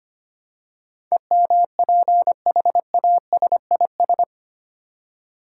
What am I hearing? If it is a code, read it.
Morse code, heard as a keyed tone, about 25 wpm, 700 Hz. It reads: EMPHASIS
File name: Eight Letter Words - 25wpm